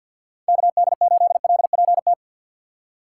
Morse code, heard as a keyed tone, about 53 wpm, 700 Hz. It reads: KD8LPT
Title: USA Call Signs - 6 Characters - 8x Character Spacing - 50wpm